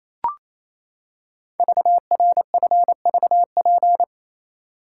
Morse code, heard as a keyed tone, about 28 wpm, 700 Hz. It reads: VRFVP
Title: Code Groups - 5 Letters - 28wpm